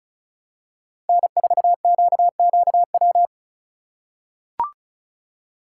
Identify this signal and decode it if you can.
Morse code, heard as a keyed tone, about 36 wpm, 700 Hz. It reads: N4QQW
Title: USA Call Signs - 5 Characters - 2x Character Spacing - 35wpm